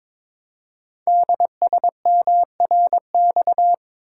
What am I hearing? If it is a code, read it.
Morse code, heard as a keyed tone, about 23 wpm, 700 Hz. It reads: DSMRX
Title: Code Groups - 5 Letters - 22wpm